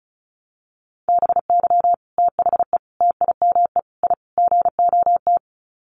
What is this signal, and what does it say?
Morse code, heard as a keyed tone, about 34 wpm, 700 Hz. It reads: BY THE TIME I GOT
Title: Sets of 5 Words Encore2 35wpm